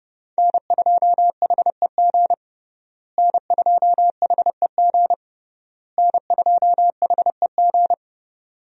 Morse code, heard as a keyed tone, about 30 wpm, 700 Hz. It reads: N2HEG N2HEG N2HEG
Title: USA Call Signs - 5 Characters - Repeated 3x - 30wpm